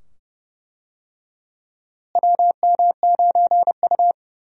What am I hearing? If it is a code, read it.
Morse code, heard as a keyed tone, about 30 wpm, 700 Hz. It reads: WM9U
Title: USA Call Signs - 4 Characters - Repeated 2x - 30wpm